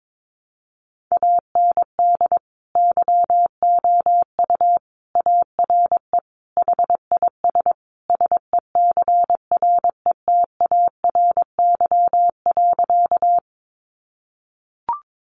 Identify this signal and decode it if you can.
Morse code, heard as a keyed tone, about 22 wpm, 700 Hz. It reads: AND YOU ARE HIS SECRETARY.
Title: Sentences from Top 1300 Words 22wpm